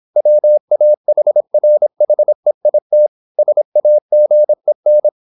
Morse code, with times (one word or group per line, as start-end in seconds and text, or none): 0.2-3.1 WAHRHEIT
3.4-5.1 SAGEN